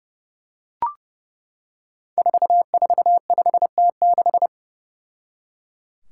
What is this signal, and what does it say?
Morse code, heard as a keyed tone, about 30 wpm, 700 Hz. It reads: 445T6
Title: Five Challenging Letter-Number Combinations 30wpm